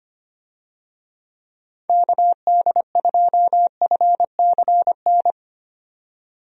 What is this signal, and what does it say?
Morse code, heard as a keyed tone, about 25 wpm, 700 Hz. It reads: KD2FCN
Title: USA Call Signs - 6 Characters - 25wpm